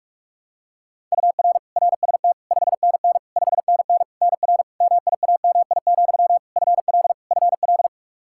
Morse code, heard as a keyed tone, about 46 wpm, 700 Hz. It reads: UR RST 5NN 5NN NR MIAMI, FL FL
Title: K1USN SST QSO Practice 45wpm